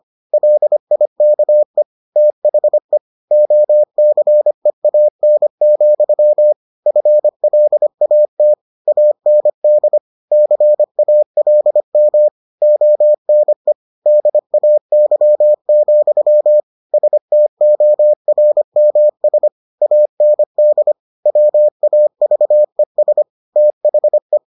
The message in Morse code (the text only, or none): LIKE THE OCEAN, FLAT AND CALM ONE DAY, STORMS AND WAVES THE